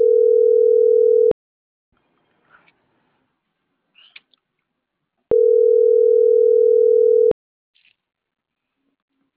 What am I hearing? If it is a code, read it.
Ringback tone (North America), 440 + 480 Hz.
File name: CENTRAL MAERSK / out polanco.msk